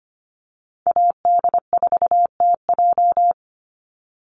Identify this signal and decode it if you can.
Morse code, heard as a keyed tone, about 25 wpm, 700 Hz. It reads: AD4TJ
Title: Call Signs Repeated 3x - 25wpm